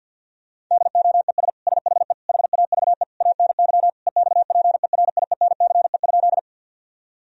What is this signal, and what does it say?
Morse code, heard as a keyed tone, about 51 wpm, 700 Hz. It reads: DOES SHE HAVE ANY EXPERIENCE?
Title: Sentences from Top 1000 Words - Review Part 3 - 50wpm